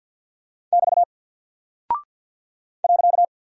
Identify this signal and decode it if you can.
Morse code, heard as a keyed tone, about 48 wpm, 700 Hz. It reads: <BT> .
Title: Single Character (Letter:Number:Punctuation:Prosign) Alt 50wpm